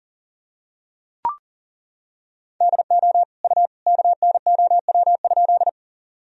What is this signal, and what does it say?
Morse code, heard as a keyed tone, about 40 wpm, 700 Hz. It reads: DO U KNOW?
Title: Sentences from Top 100 Words and Abbreviations 40wpm